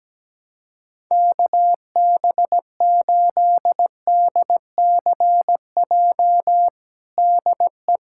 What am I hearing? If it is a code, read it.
Morse code, heard as a keyed tone, about 17 wpm, 700 Hz. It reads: KB8DCJ DE
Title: USA Callsign DE Callsign - 6 Characters - 17wpm